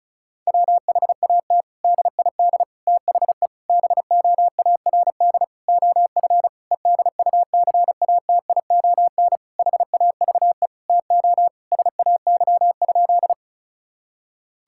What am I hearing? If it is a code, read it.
Morse code, heard as a keyed tone, about 34 wpm, 700 Hz. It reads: WHAT DID THE BOARD OF EDUCATION HAVE TO SAY?
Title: Sentences from Top 1000 Words - Review Part 6 - 35wpm